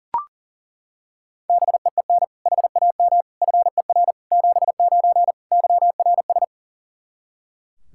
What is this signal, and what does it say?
Morse code, heard as a keyed tone, about 37 wpm, 700 Hz. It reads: BEEN HAM FER 79 YRS
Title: Practice QSO - Exchange 3 - More Info - 40wpm